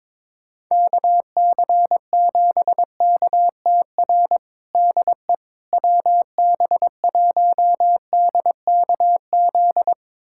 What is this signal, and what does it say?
Morse code, heard as a keyed tone, about 22 wpm, 700 Hz. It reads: KC7KTR DE WB1DKZ
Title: USA Callsign DE Callsign 22wpm